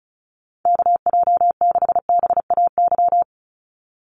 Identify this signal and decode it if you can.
Morse code, heard as a keyed tone, about 34 wpm, 700 Hz. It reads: KJ6BAY
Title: USA Call Signs 35wpm